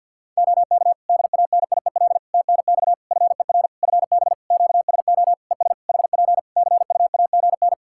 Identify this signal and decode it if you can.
Morse code, heard as a keyed tone, about 49 wpm, 700 Hz. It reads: OK DANIEL TNX FER FB QSO ES HP CUAGN
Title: K1USN SST QSO Practice 50wpm